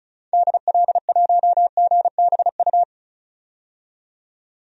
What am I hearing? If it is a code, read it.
Morse code, heard as a keyed tone, about 34 wpm, 700 Hz. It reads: DL1GBU
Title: DXCC-100 Call Signs - 6 Characters - 35wpm